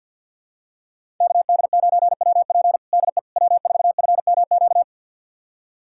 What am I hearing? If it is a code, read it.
Morse code, heard as a keyed tone, about 52 wpm, 700 Hz. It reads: KD9WP DE W4FMQ
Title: USA Callsign DE Callsign - 5 Characters - 50wpm